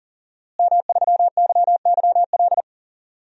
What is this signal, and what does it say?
Morse code, heard as a keyed tone, about 37 wpm, 700 Hz. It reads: M3YYL
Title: DXCC-100 Call Signs - 5 Characters - Repeated 2x - 40wpm